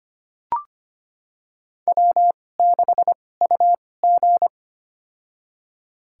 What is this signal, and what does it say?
Morse code, heard as a keyed tone, about 25 wpm, spaced as for 18 wpm, 700 Hz. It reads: W6UG
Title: USA Call Signs - 4 Characters - 2x Character Spacing - 25wpm